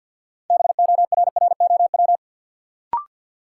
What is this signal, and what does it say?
Morse code, heard as a keyed tone, about 53 wpm, 700 Hz. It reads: BORROW
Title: Sentences from Top 1500 Words - New Word Drill - 50wpm